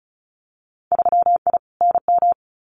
Morse code, heard as a keyed tone, about 34 wpm, 700 Hz. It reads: 3I NM
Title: Winter Field Day Exchanges Encore 35wpm